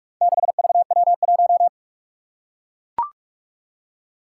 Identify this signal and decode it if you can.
Morse code, heard as a keyed tone, about 44 wpm, 700 Hz. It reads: 6VW1
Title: Four Challenging Letter-Number Combinations 45wpm